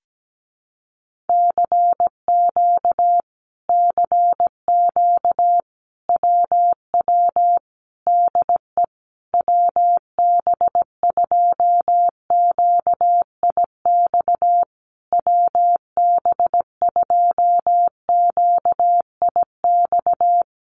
Morse code, heard as a keyed tone, about 17 wpm, 700 Hz. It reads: CQ CQ WW DE WB2QIX WB2QIX
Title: Common QSO Phrases Encore 17wpm